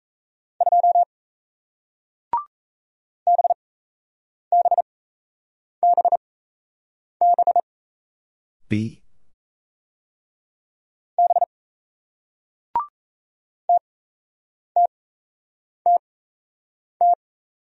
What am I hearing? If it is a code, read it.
Morse code, heard as a keyed tone, about 38 wpm, spaced as for 8 wpm, 700 Hz. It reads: 2 BBBB B TTTT